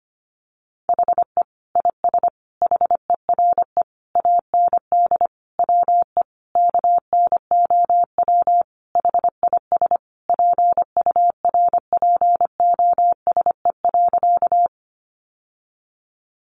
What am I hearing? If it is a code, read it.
Morse code, heard as a keyed tone, about 25 wpm, 700 Hz. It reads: HE IS HERE AND WE KNOW HIS PURPOSE.
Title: Sentences from Top 800 Words 25wpm